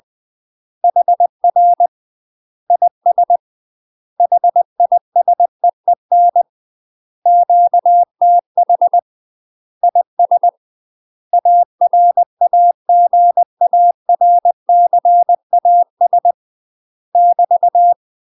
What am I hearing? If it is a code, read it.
Morse code, heard as a keyed tone, about 20 wpm, 700 Hz. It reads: HR IS HISEEN QTH IS ARAGARCAS <BT>
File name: UK MASTER QSOs TRAINING / 20WPM / UK MASTER QSOs 0051